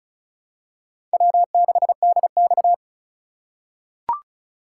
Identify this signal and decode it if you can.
Morse code, heard as a keyed tone, about 36 wpm, 700 Hz. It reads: W6DX
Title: USA Call Signs - 4 Characters - 3x Character Spacing - 35wpm